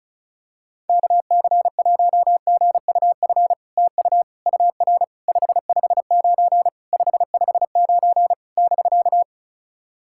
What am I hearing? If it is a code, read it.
Morse code, heard as a keyed tone, about 34 wpm, 700 Hz. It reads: KC1GUF TU UR 559 559 <BK>
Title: SOTA QSO Practice 35wpm